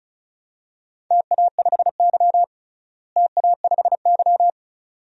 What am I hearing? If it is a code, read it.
Morse code, heard as a keyed tone, about 36 wpm, 700 Hz. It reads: TA5Y TA5Y
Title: DXCC-100 Call Signs - 4 Characters - Repeated 2x - 35wpm